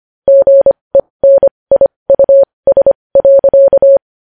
Morse code, heard as a keyed tone, about 25 wpm, spaced as for 20 wpm, 550 Hz. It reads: GENIUS.
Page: News Headlines in Morse Code at 20 WPM